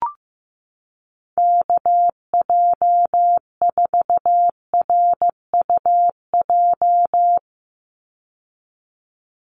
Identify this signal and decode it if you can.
Morse code, heard as a keyed tone, about 15 wpm, 700 Hz. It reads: KJ4RUJ
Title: USA Call Signs 15wpm